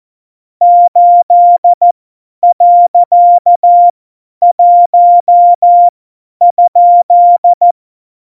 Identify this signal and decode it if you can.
Morse code, heard as a keyed tone, about 14 wpm, spaced as for 10 wpm, 700 Hz. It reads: 8.1?
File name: Morsekurs / Witze / Witz 15. Januar 2019, 50 F70